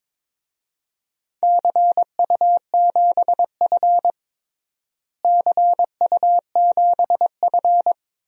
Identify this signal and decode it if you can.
Morse code, heard as a keyed tone, about 22 wpm, 700 Hz. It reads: CU7F CU7F
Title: DXCC-100 Call Signs - 4 Characters - Repeated 2x - 22wpm